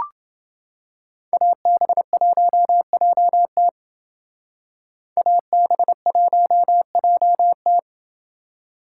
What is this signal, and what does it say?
Morse code, heard as a keyed tone, about 28 wpm, 700 Hz. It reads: AB1JT AB1JT